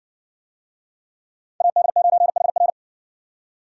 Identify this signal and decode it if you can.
Morse code, heard as a keyed tone, about 57 wpm, 700 Hz. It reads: AD9HR